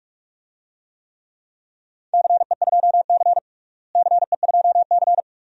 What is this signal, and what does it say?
Morse code, heard as a keyed tone, about 43 wpm, 700 Hz. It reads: CE2C CE2C
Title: DXCC-100 Call Signs - 4 Characters - Repeated 2x - 45wpm